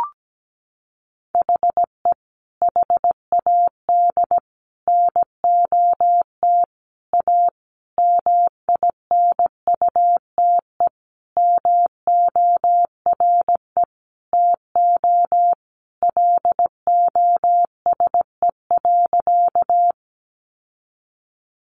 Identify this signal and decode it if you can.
Morse code, heard as a keyed tone, about 17 wpm, 700 Hz. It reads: HE HAD NOT A MINUTE MORE TO LOSE.
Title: Sentences from Top 1100 Words 17wpm